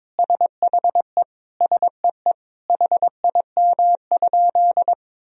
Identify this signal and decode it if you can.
Morse code, heard as a keyed tone, about 22 wpm, 700 Hz. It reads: SHE SEE HIM?
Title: Sentences from Top 100 Words and Abbreviations 22wpm